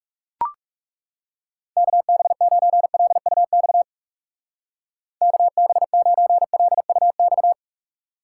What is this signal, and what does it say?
Morse code, heard as a keyed tone, about 36 wpm, 700 Hz. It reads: KB9LUX KB9LUX
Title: Speed Racing - USA 2x3 Call Signs 30wpm